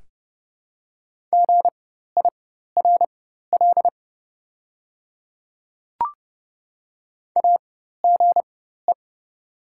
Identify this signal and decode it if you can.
Morse code, heard as a keyed tone, about 30 wpm, spaced as for 14 wpm, 700 Hz. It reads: GIRL AGE
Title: 4x Character Spacing - Top 500 Words - 30wpm